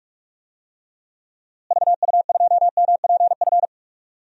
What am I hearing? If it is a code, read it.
Morse code, heard as a keyed tone, about 46 wpm, 700 Hz. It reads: VU2MPF
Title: DXCC-100 Call Signs - 6 Characters - Repeated 2x - 45wpm